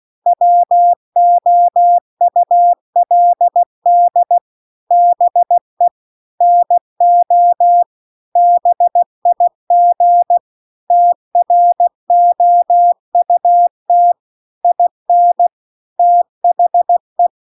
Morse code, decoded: WOULD BE NO BIG TROUT IN THE

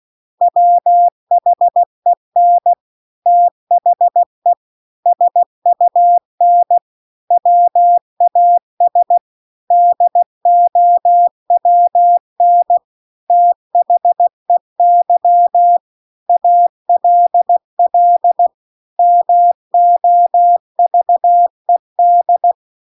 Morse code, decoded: WHEN THE SUN WAS DOWN THEY ALL MOVED

